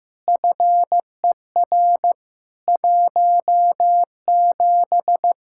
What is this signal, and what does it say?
Morse code, heard as a keyed tone, about 16 wpm, 700 Hz. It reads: FER 17
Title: Practice QSO - Exchange 3 - More Info - 15wpm